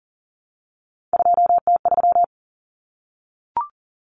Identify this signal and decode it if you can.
Morse code, heard as a keyed tone, about 40 wpm, 700 Hz. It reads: 2T3
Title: Common Cut Numbers - 3-Digit 40wpm